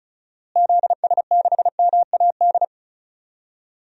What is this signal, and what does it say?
Morse code, heard as a keyed tone, about 36 wpm, 700 Hz. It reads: ZS6MAD
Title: DXCC-100 Call Signs - 6 Characters - Repeated 2x - 35wpm